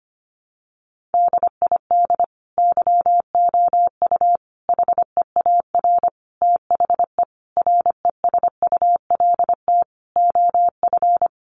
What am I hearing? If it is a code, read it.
Morse code, heard as a keyed tone, about 25 wpm, 700 Hz. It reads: DID YOU HEAR THE RESULT OF
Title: Sentences from Top 1400 Words 25wpm